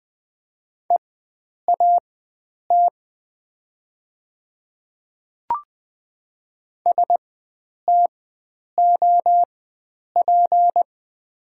Morse code, heard as a keyed tone, about 20 wpm, spaced as for 9 wpm, 700 Hz. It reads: EAT STOP
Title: 4x Character Spacing - Top 500 Words - 20wpm